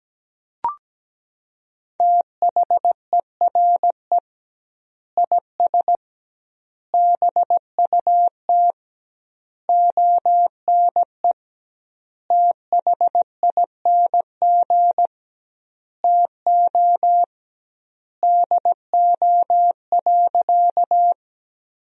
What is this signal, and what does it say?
Morse code, heard as a keyed tone, about 17 wpm, 700 Hz. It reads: THERE IS BUT ONE THING TO DO.
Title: 2x Word-Spacing - Sentences from Top 100 Words - 17wpm